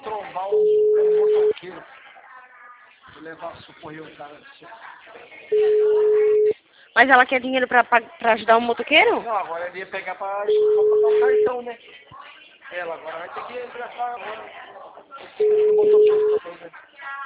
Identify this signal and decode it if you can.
Ringback tone (Europe, 425 Hz), with speech.